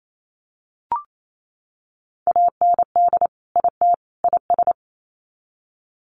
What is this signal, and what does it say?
Morse code, heard as a keyed tone, about 28 wpm, 700 Hz. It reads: AND IT IS
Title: Sets of 3 Words 28wpm